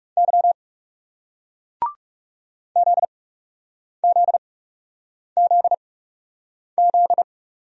Morse code, heard as a keyed tone, about 42 wpm, spaced as for 8 wpm, 700 Hz. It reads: Y ZZZZ